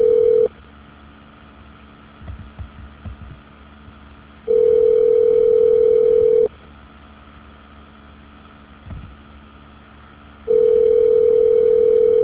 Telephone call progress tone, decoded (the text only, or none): ringback tone